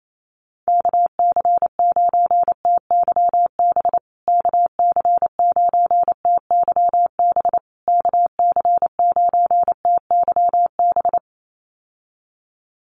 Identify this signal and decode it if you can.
Morse code, heard as a keyed tone, about 28 wpm, 700 Hz. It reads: KC9TYB KC9TYB KC9TYB
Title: USA Call Signs Repeated 3x 28wpm